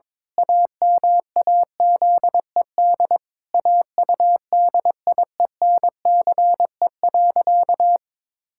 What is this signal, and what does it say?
Morse code, heard as a keyed tone, about 22 wpm, 700 Hz. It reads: AMAZED AUDIENCE.
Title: Pangram Challenge v2 - 22wpm